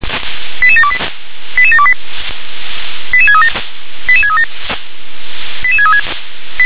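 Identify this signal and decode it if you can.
Five-tone selective call (ZVEI), heard as quick runs of five tones, tones of about 70 ms. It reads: 88728 88728 88637 88637 88636
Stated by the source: ZVEI five-tone selective call 88149, received with an SDR.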